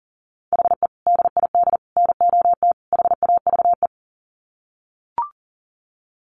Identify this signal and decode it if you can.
Morse code, heard as a keyed tone, about 40 wpm, 700 Hz. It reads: HE DID NOT HAVE
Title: Sets of 4 Words 40wpm